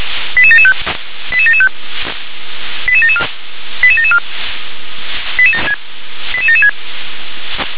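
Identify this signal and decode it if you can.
Five-tone selective call (ZVEI), heard as quick runs of five tones, tones of about 70 ms. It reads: 88775 88775 88774 88774 88776
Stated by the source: ZVEI five-tone selective call received by SDR.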